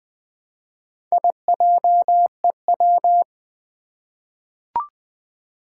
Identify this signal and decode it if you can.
Morse code, heard as a keyed tone, about 20 wpm, 700 Hz. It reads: IJEW
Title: Four Challenging Letter-Number Combinations 20wpm